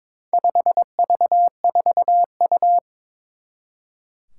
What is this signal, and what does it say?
Morse code, heard as a keyed tone, about 22 wpm, 700 Hz. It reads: 5V4U
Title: Four Challenging Letter-Number Combinations 22wpm